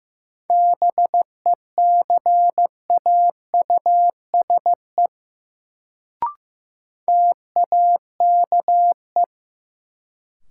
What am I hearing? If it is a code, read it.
Morse code, heard as a keyed tone, about 15 wpm, 700 Hz. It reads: BECAUSE TAKE
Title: Sets of 2 Words - New Word Drill - 15wpm